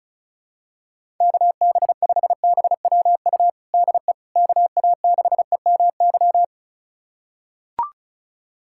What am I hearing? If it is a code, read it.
Morse code, heard as a keyed tone, about 34 wpm, 700 Hz. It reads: KB5BWU DE KA6EMY
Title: USA Callsign DE Callsign - 6 Characters - 35wpm